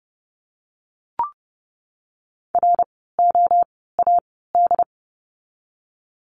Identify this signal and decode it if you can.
Morse code, heard as a keyed tone, about 30 wpm, spaced as for 17 wpm, 700 Hz. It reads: ROAD
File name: Lesson 17f - Letter u - Words 30wpm